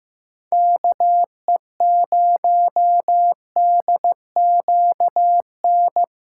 Morse code, heard as a keyed tone, about 15 wpm, 700 Hz. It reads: KE0DQN
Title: USA Callsign DE Callsign - 6 Characters - 15wpm